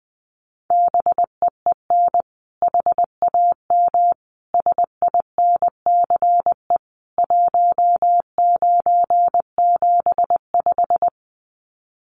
Morse code, heard as a keyed tone, about 19 wpm, 700 Hz. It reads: BEEN HAM SINCE 1975
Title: Common QSO Phrases 20wpm